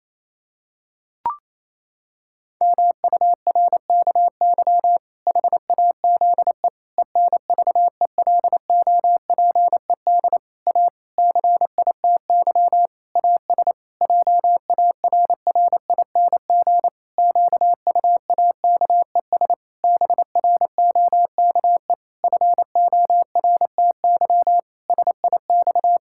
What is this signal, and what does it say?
Morse code, heard as a keyed tone, about 28 wpm, 700 Hz. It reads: MURKY HAZE ENVELOPED A CITY AS JARRING QUAKES BROKE FORTY SIX
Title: Pangram Challenge v2 - 28wpm